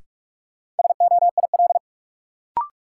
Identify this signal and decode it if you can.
Morse code, heard as a keyed tone, about 46 wpm, 700 Hz. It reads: SOIL
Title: Sentences from Top 1500 Words - New Word Drill Alt - 45wpm